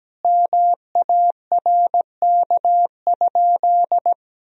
Morse code, heard as a keyed tone, about 17 wpm, 700 Hz. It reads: MARK?
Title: Sentences from Top 1000 Words - Review Part 1 - 17wpm